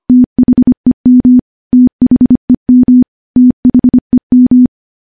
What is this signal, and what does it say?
Morse code, heard as a keyed tone, about 25 wpm, 250 Hz. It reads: THEM THEM THEM
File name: morse / morse-training-25WPM